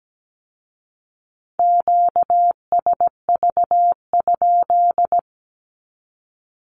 Morse code, heard as a keyed tone, about 17 wpm, 700 Hz. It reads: QSV?
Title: Q-Codes with Spoken Meaning - 17wpm